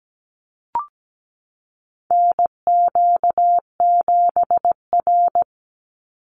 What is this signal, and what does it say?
Morse code, heard as a keyed tone, about 17 wpm, 700 Hz. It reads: NQ7R
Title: NA SOTA Chaser Callsigns 17wpm